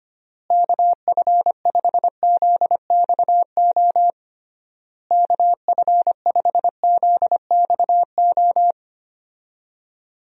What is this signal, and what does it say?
Morse code, heard as a keyed tone, about 25 wpm, 700 Hz. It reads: KF5ZXO KF5ZXO
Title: USA Call Signs - 6 Characters - Repeated 3x - 25wpm